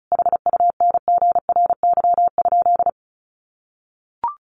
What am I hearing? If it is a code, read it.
Morse code, heard as a keyed tone, about 34 wpm, 700 Hz. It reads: HUNGRY?
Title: Sentences from Top 1400 Words 35wpm